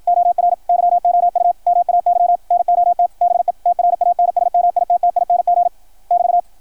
Morse code, heard as a keyed tone, about 53 wpm, 690 Hz, with radio noise. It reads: OR YOU MAY NOT BE TRANSMITTING <BT>
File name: archived / CRSnet